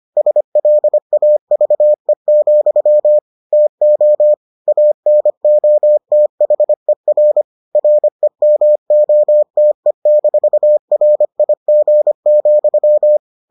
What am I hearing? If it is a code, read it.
Morse code, heard as a keyed tone, about 25 wpm, 600 Hz. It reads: SLAVE, TO ANOTHER REMOTE-RIG,